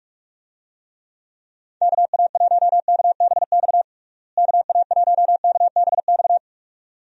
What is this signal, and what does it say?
Morse code, heard as a keyed tone, about 46 wpm, 700 Hz. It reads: KA1KBX KA1KBX
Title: USA Call Signs - 6 Characters - Repeated 2x - 45wpm